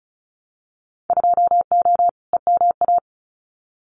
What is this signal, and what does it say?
Morse code, heard as a keyed tone, about 34 wpm, 700 Hz. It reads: 2O EMA
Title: Winter Field Day Exchanges 35wpm